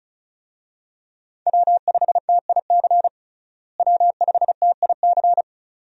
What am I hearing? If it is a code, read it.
Morse code, heard as a keyed tone, about 34 wpm, 700 Hz. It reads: W5TIC W5TIC
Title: USA Call Signs - 5 Characters - Repeated 2x - 35wpm